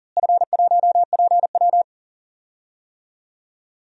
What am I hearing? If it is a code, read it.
Morse code, heard as a keyed tone, about 42 wpm, 690 Hz. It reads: F1PW
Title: DXCC-100 Call Signs - 4 Characters - Repeated 2x - 40wpm